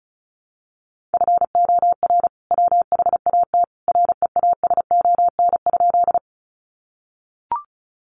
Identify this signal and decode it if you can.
Morse code, heard as a keyed tone, about 34 wpm, 700 Hz. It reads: FOR WHAT REASON?
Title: Sentences from Top 700 Words 35wpm